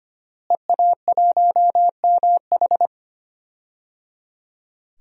Morse code, heard as a keyed tone, about 25 wpm, 700 Hz. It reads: EA1MH
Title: DXCC-100 Call Signs - 5 Characters - Repeated 3x - 25wpm